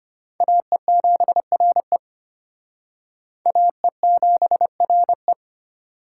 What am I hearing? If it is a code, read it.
Morse code, heard as a keyed tone, about 31 wpm, 700 Hz. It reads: AE7RE AE7RE
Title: Speed Racing - USA 2x2 Call Signs 25wpm